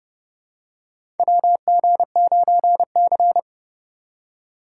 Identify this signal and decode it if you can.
Morse code, heard as a keyed tone, about 30 wpm, 700 Hz. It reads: WG9C
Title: USA Call Signs - 4 Characters - 4x Character Spacing - 30wpm